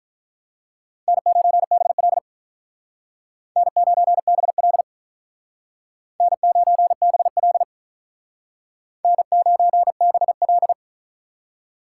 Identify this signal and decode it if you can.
Morse code, heard as a keyed tone, about 45 wpm, 700 Hz. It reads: N9BL N9BL N9BL N9BL